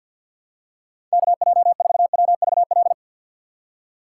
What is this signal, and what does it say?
Morse code, heard as a keyed tone, about 53 wpm, 700 Hz. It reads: KJ4WVL
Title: USA Call Signs - 6 Characters - 3x Character Spacing - 50wpm